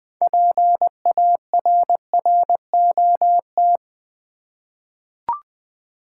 Morse code, heard as a keyed tone, about 20 wpm, 690 Hz. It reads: PARROT
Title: Easy Words with Hints - 20wpm